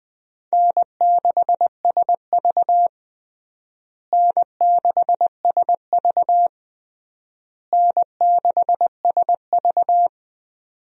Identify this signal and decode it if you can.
Morse code, heard as a keyed tone, about 20 wpm, 700 Hz. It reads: N6SV N6SV N6SV
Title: USA Call Signs - 4 Characters - Repeated 3x - 20wpm